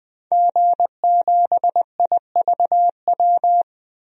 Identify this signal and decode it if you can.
Morse code, heard as a keyed tone, about 20 wpm, 700 Hz. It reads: G7IVW
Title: DXCC-100 Call Signs - 5 Characters - Repeated 3x - 20wpm